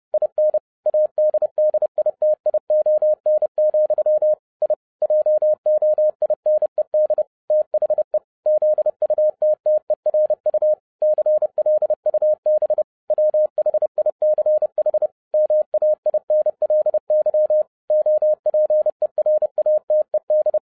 Morse code, heard as a keyed tone, about 30 wpm, 610 Hz. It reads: IN ADDITION, I JOINED THE ZUTTERU CLUB WHICH MAINLY OPERATED